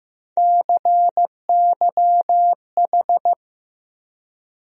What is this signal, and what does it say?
Morse code, heard as a keyed tone, about 15 wpm, 700 Hz. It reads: CYH
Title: USA Call Sign Postfixes - 3 Letters 15wpm